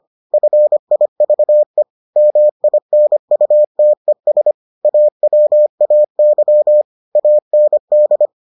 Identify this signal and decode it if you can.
Morse code, heard as a keyed tone, about 25 wpm, 600 Hz. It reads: FIVE MINUTES AWAY AND